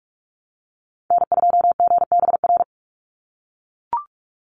Morse code, heard as a keyed tone, about 44 wpm, 700 Hz. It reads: N2GBR
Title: NA SOTA Activator Callsigns 45wpm